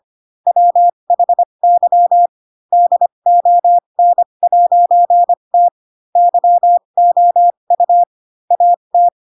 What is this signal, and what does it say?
Morse code, heard as a keyed tone, about 25 wpm, spaced as for 22 wpm, 700 Hz. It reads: WHY DON'T YOU AT
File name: Pinocchio / Pinocchio ch4 18 25 ik1hge